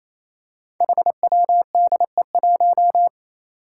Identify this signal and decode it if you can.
Morse code, heard as a keyed tone, about 28 wpm, 700 Hz. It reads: HWDE1
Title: Five Challenging Letter-Number Combinations 28wpm